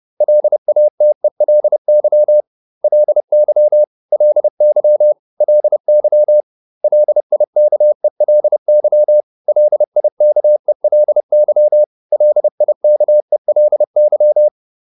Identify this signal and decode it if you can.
Morse code, heard as a keyed tone, about 30 wpm, 600 Hz. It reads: LATELY LY LY LY LIKELY LIKELY LIKELY